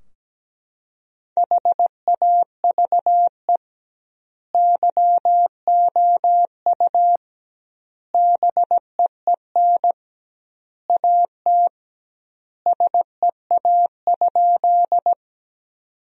Morse code, heard as a keyed tone, about 17 wpm, 700 Hz. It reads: HAVE YOU BEEN AT SEA?
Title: 2x Word-Spacing - Sentences from Top 300 Words - 17wpm